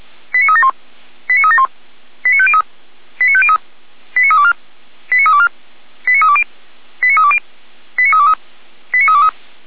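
Five-tone selective call (ZVEI), heard as quick runs of five tones, tones of about 70 ms. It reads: 89371 89371 89503 89503 89324 89324 89320 89320 89323 89323
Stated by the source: ZVEI five-tone selective call received by SDR, fire brigade Orlamünde (feuerwehr-orlamuende).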